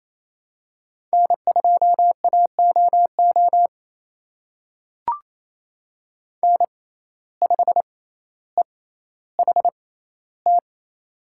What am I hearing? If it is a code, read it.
Morse code, heard as a keyed tone, about 28 wpm, spaced as for 14 wpm, 700 Hz. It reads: N2AOO N5EHT
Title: USA Call Signs - 5 Characters - 6x Character Spacing - 28wpm